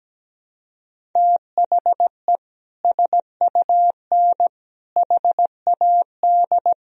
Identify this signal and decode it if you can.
Morse code, heard as a keyed tone, about 17 wpm, 700 Hz. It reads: THE SUN HAD
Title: Sentences from Top 300 Words and Abbreviations 17wpm